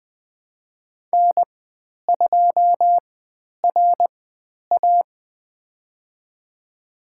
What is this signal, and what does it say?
Morse code, heard as a keyed tone, about 20 wpm, spaced as for 10 wpm, 700 Hz. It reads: N2RA
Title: 095 Lesson 16g - Letter c - Call Signs 20wpm